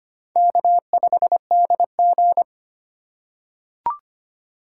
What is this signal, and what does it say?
Morse code, heard as a keyed tone, about 25 wpm, 700 Hz. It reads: K5DG
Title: USA Call Signs - 4 Characters - 4x Character Spacing - 25wpm